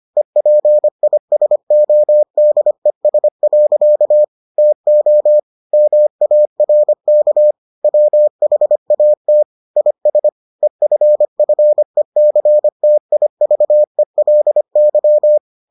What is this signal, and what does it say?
Morse code, heard as a keyed tone, about 25 wpm, 600 Hz. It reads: EPISODES. TO MARK WHAT IS EFFECTIVELY